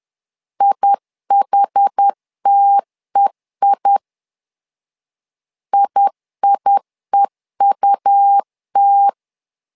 Morse code, heard as a keyed tone, about 10 wpm, 800 Hz. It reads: IHTEI IIEUT
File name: LEZIONI / 2 UHDN / 2 UHD 10wpm 1826